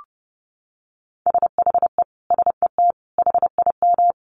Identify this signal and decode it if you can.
Morse code, heard as a keyed tone, about 30 wpm, 700 Hz. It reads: SHE SET HIM
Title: Sentences from Top 1500 Words 30wpm